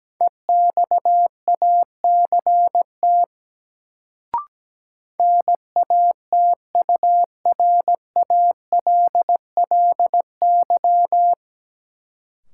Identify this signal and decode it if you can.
Morse code, heard as a keyed tone, about 17 wpm, 700 Hz. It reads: EXACT NATURALLY
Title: Sentences from Top 1300 Words - New Word Drill - 17wpm